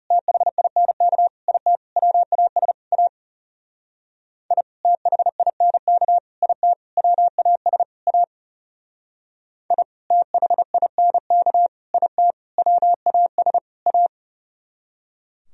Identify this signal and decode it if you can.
Morse code, heard as a keyed tone, about 37 wpm, 700 Hz. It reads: THINK IT WAS A I THINK IT WAS A I THINK IT WAS A
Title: Speed-Racing - Sets of 5 Words 30wpm